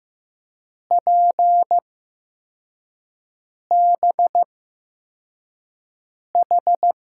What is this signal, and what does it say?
Morse code, heard as a keyed tone, about 15 wpm, spaced as for 4 wpm, 700 Hz. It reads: PBH
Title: USA Call Signs - 6 Characters - 8x Character Spacing - 15wpm